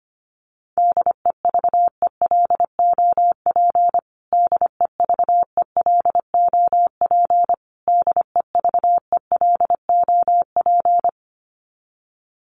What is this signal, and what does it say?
Morse code, heard as a keyed tone, about 25 wpm, 700 Hz. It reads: DEVELOP DEVELOP DEVELOP
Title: Top 500 Words Repeated 25wpm